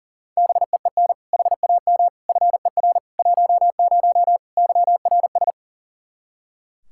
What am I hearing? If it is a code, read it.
Morse code, heard as a keyed tone, about 40 wpm, 700 Hz. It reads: BEEN HAM FER 10 YRS
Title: Practice QSO - Exchange 3 - More Info - 40wpm